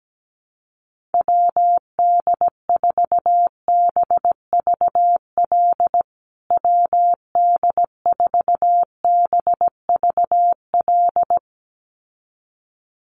Morse code, heard as a keyed tone, about 17 wpm, 700 Hz. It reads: WD4BVL WD4BVL
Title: USA Call Signs Repeated 2x 17wpm